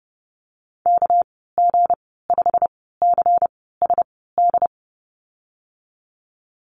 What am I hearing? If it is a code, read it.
Morse code, heard as a keyed tone, about 30 wpm, spaced as for 17 wpm, 700 Hz. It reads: KG5CSD
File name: Lesson 37g - Letter x - Call Signs 30wpm